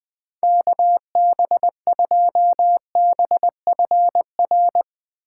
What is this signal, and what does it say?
Morse code, heard as a keyed tone, about 20 wpm, 700 Hz. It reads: KB2BFR